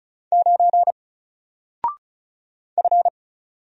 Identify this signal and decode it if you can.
Morse code, heard as a keyed tone, about 36 wpm, 700 Hz. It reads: 9 F
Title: Single Letter-Number Alt 35wpm